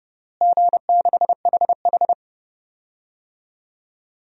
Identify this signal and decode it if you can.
Morse code, heard as a keyed tone, about 30 wpm, 700 Hz. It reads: G6HH
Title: DXCC-100 Call Signs - 4 Characters - Repeated 3x - 30wpm